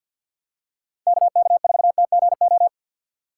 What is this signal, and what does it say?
Morse code, heard as a keyed tone, about 53 wpm, 700 Hz. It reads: KK4TGO
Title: USA Call Signs - 6 Characters - 6x Character Spacing - 50wpm